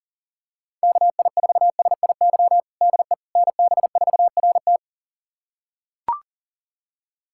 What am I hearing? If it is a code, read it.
Morse code, heard as a keyed tone, about 40 wpm, 700 Hz. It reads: KI4SIY DE NB4RT
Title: USA Callsign DE Callsign 40wpm